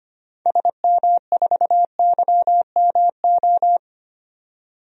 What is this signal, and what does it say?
Morse code, heard as a keyed tone, about 25 wpm, 700 Hz. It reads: SM4YMO
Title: DXCC-100 Call Signs - 6 Characters - 25wpm